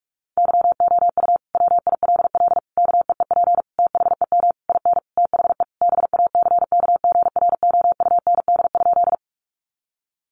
Morse code, heard as a keyed tone, about 44 wpm, 700 Hz. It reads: YOU WILL KEEP THEM IN THE BACKGROUND?
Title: Sentences from Top 1300 Words 45wpm